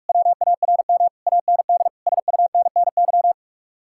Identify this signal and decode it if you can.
Morse code, heard as a keyed tone, about 44 wpm, 700 Hz. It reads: WARM AND SUNNY